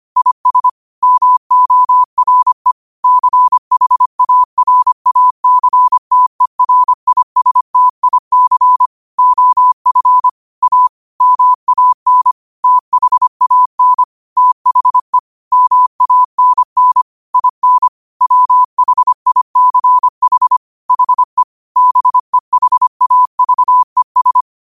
Morse code, heard as a keyed tone, about 25 wpm, 1000 Hz. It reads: IS MORE CHARACTERISTIC OF A MAN THAN THE MANN IN WHICH HE BEHAVES
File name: qotdfiles / 25wpm